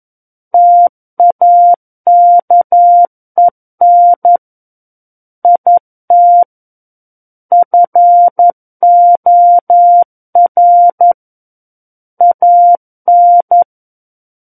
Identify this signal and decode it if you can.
Morse code, heard as a keyed tone, about 11 wpm, 700 Hz. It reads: TAKEN IT FOR AN
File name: 11 WPM